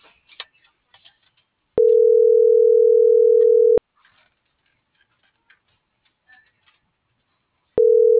Ringback tone (North America), 440 + 480 Hz.